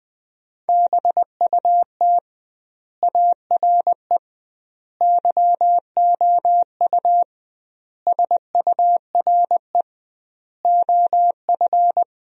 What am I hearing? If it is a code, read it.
Morse code, heard as a keyed tone, about 20 wpm, 700 Hz. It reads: BUT ARE YOU SURE OF
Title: 2x Word-Spacing - Sentences from Top 300 Words - 20wpm